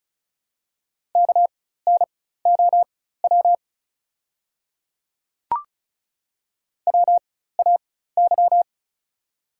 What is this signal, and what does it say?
Morse code, heard as a keyed tone, about 36 wpm, spaced as for 16 wpm, 700 Hz. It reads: KNOW WAY
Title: 4x Character Spacing - Top 100 Words - 35wpm